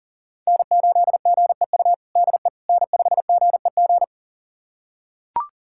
Morse code, heard as a keyed tone, about 40 wpm, 700 Hz. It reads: N8GEU DE N5GEG
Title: USA Callsign DE Callsign - 5 Characters - 40wpm